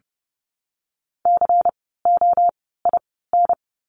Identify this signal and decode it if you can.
Morse code, heard as a keyed tone, about 30 wpm, 700 Hz. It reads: C O I N
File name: Lesson 16f - Letter c - Words 30wpm